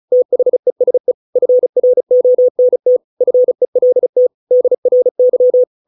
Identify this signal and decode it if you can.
Morse code, heard as a keyed tone, about 34 wpm, 500 Hz. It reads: THESE FRONT FELT DRY